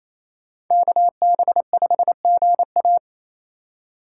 Morse code, heard as a keyed tone, about 34 wpm, 700 Hz. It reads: KB5GA
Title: Speed Racing - USA Call Signs 28wpm